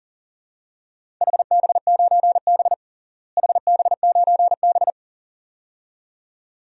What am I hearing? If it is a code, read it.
Morse code, heard as a keyed tone, about 40 wpm, 700 Hz. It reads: HB9B HB9B
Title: DXCC-100 Call Signs - 4 Characters - Repeated 2x - 40wpm